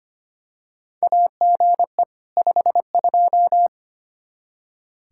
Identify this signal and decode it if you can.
Morse code, heard as a keyed tone, about 25 wpm, 700 Hz. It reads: AGE 52
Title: Practice QSO - Exchange 3 - More Info - 25wpm